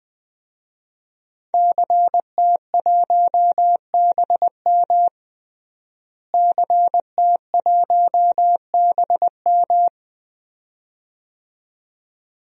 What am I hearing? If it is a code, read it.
Morse code, heard as a keyed tone, about 20 wpm, 700 Hz. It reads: CT1BM CT1BM
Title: DXCC-100 Call Signs - 5 Characters - Repeated 2x - 20wpm